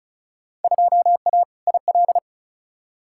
Morse code, heard as a keyed tone, about 36 wpm, 700 Hz. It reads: 2A IL
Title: Field Day Exchanges - 35wpm